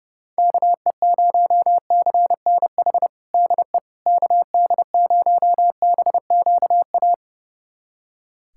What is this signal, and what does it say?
Morse code, heard as a keyed tone, about 30 wpm, 700 Hz. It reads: KE0CNH DE KD0BQA
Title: USA Callsign DE Callsign - 6 Characters - 30wpm